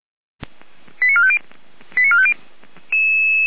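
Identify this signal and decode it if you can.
Five-tone selective call (ZVEI), heard as quick runs of five tones, tones of about 70 ms. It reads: 89450 89450
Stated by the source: ZVEI five-tone call 88029 received by SDR at Feuerwehr Orlamünde.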